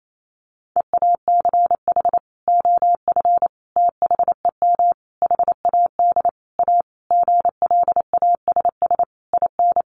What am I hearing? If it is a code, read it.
Morse code, heard as a keyed tone, about 28 wpm, 700 Hz. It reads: EACH OF THEM HAD A GLASS IN
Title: Sentences from Top 1300 Words 28wpm